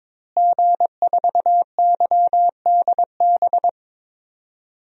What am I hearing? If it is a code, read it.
Morse code, heard as a keyed tone, about 21 wpm, 700 Hz. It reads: G4YDB
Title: DXCC-100 Call Signs - 5 Characters - Repeated 2x - 22wpm